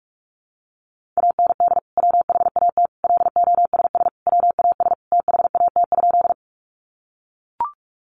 Morse code, heard as a keyed tone, about 44 wpm, 700 Hz. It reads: AND WHAT LOSS WAS THAT?
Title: Sentences from Top 1400 Words 45wpm